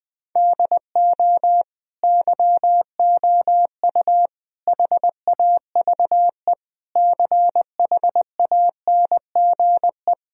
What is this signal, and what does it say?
Morse code, heard as a keyed tone, about 20 wpm, 700 Hz. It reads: DO YOU HAVE CHANGE
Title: Sentences from Top 1000 Words - Review Part 2 - 20wpm